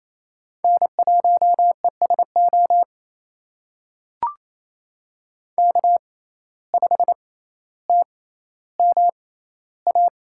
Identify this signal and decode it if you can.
Morse code, heard as a keyed tone, about 28 wpm, spaced as for 14 wpm, 700 Hz. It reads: N1ESO K5TMA
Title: USA Call Signs - 5 Characters - 6x Character Spacing - 28wpm